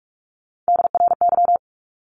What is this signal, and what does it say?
Morse code, heard as a keyed tone, about 44 wpm, 700 Hz. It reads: DRY
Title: Top 500 Words Repeated 45wpm